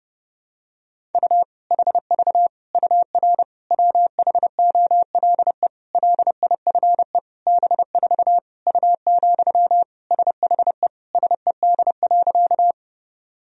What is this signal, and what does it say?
Morse code, heard as a keyed tone, about 32 wpm, 700 Hz. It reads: U HV UR WHOLE LIFE B4 U, SHE SED.
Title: Sentences from Top 400 Words and Abbreviations 30wpm